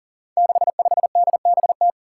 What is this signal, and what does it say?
Morse code, heard as a keyed tone, about 41 wpm, 700 Hz. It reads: 65DBT
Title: Five Challenging Letter-Number Combinations 40wpm